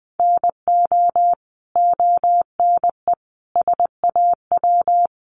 Morse code, heard as a keyed tone, about 20 wpm, 700 Hz. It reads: NO ONE SAW
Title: Sentences from Top 1200 Words 20wpm